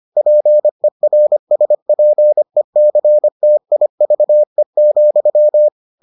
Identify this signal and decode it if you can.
Morse code, heard as a keyed tone, about 25 wpm, 600 Hz. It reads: PERSPECTIVE,